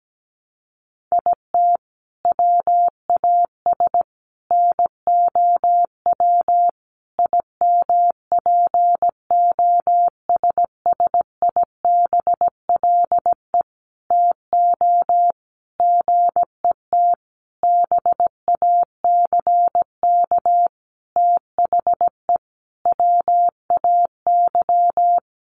Morse code, heard as a keyed tone, about 17 wpm, 700 Hz. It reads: IT WAS NOW IMPOSSIBLE TO GET BACK THE WAY
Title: Sentences from Top 1100 Words 17wpm